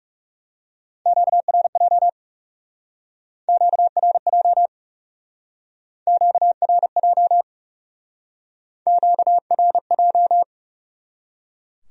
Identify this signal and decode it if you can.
Morse code, heard as a keyed tone, about 40 wpm, 700 Hz. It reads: QRJ QRJ QRJ QRJ